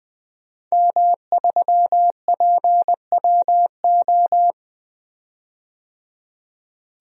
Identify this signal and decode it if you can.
Morse code, heard as a keyed tone, about 20 wpm, 700 Hz. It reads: M3PWO
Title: DXCC-100 Call Signs - 5 Characters - 20wpm